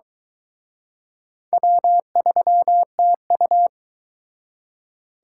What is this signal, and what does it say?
Morse code, heard as a keyed tone, about 23 wpm, 700 Hz. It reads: W3TU